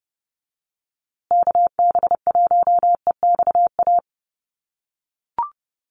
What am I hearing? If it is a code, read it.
Morse code, heard as a keyed tone, about 30 wpm, 700 Hz. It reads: KB1EXA
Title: USA Call Signs Repeated 3x 30wpm